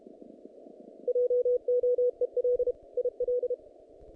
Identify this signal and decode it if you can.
Morse code, heard as a keyed tone, about 32 wpm, 490 Hz. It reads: JOEL IL
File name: contests / naqp-cw-k5tr-op-k5pi- -2100